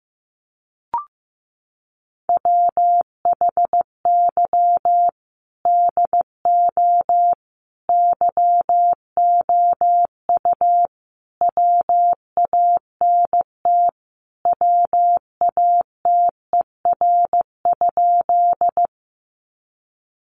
Morse code, heard as a keyed tone, about 14 wpm, 700 Hz. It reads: WHY DO YOU WANT WATER?
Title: Sentences from Top 200 Words 15wpm